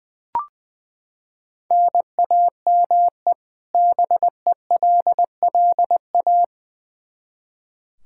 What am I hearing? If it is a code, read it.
Morse code, heard as a keyed tone, about 21 wpm, 700 Hz. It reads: NAME BELLA
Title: Practice QSO - Exchange 1 - Beginning - 20wpm